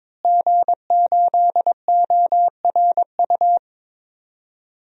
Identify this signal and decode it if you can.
Morse code, heard as a keyed tone, about 23 wpm, 700 Hz. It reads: G8ORU
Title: DXCC-100 Call Signs - 5 Characters - Repeated 2x - 22wpm